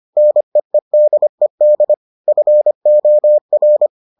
Morse code, heard as a keyed tone, about 25 wpm, 600 Hz. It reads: NEEDED FOR